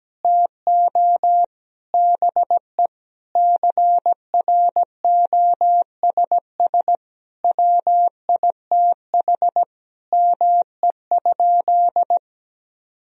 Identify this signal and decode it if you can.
Morse code, heard as a keyed tone, about 17 wpm, 700 Hz. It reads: TO BE CROSS WITH ME?
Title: Sentences from Top 1000 Words - Review Part 1 - 17wpm